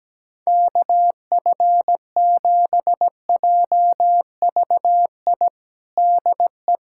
Morse code, heard as a keyed tone, about 17 wpm, 700 Hz. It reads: KF7JVI DE
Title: USA Callsign DE Callsign - 6 Characters - 17wpm